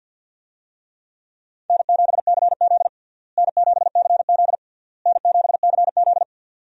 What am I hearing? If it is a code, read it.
Morse code, heard as a keyed tone, about 51 wpm, 700 Hz. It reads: N7CZ N7CZ N7CZ
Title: USA Call Signs - 4 Characters - Repeated 3x - 50wpm